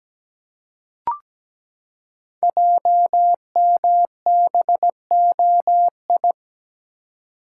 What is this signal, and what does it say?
Morse code, heard as a keyed tone, about 17 wpm, 700 Hz. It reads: JMBOI
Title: Code Groups - 5 Letters - 17wpm